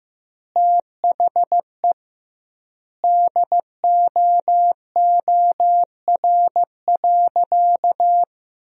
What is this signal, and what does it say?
Morse code, heard as a keyed tone, about 15 wpm, 700 Hz. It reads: THE DOOR.
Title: 2x Word-Spacing - Sentences from Top 300 Words - 15wpm